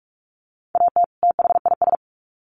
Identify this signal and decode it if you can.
Morse code, heard as a keyed tone, about 44 wpm, 700 Hz. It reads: AT THIS
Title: Sets of 2 Words 45wpm